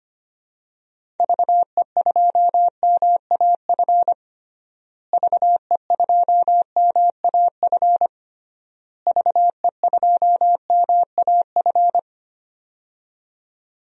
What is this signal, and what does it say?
Morse code, heard as a keyed tone, about 25 wpm, 700 Hz. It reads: VE2MAF VE2MAF VE2MAF
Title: DXCC-100 Call Signs - 6 Characters - Repeated 3x - 25wpm